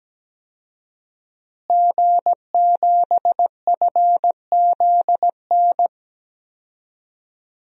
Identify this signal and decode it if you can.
Morse code, heard as a keyed tone, about 17 wpm, 700 Hz. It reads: G7FZN
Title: DXCC-100 Call Signs - 5 Characters - Repeated 2x - 17wpm